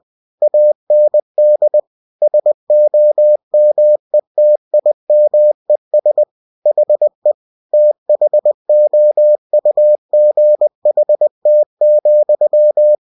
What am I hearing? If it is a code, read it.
Morse code, heard as a keyed tone, about 20 wpm, 600 Hz. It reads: AND SOMETIMES HE THOUGHT,
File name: cw-books / winnie-the-pooh / winnie-20wpm-page-0026